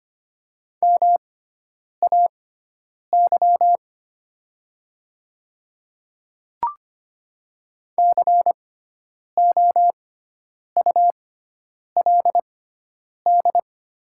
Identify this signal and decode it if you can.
Morse code, heard as a keyed tone, about 25 wpm, spaced as for 9 wpm, 700 Hz. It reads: MAY COULD
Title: 6x Character Spacing - Top 200 Words - 25wpm